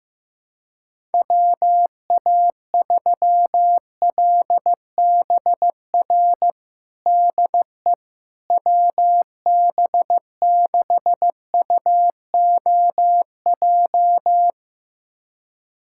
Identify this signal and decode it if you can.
Morse code, heard as a keyed tone, about 16 wpm, 700 Hz. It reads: WA3LBR DE WB6UOJ
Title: USA Callsign DE Callsign - 6 Characters - 15wpm